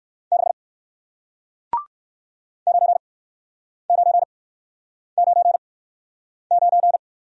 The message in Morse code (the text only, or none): 6 9999